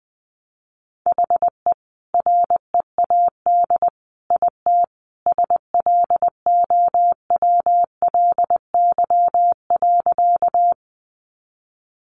Morse code, heard as a keyed tone, about 20 wpm, 700 Hz. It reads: HE READ IT SLOWLY.
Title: Sentences from Top 1100 Words 20wpm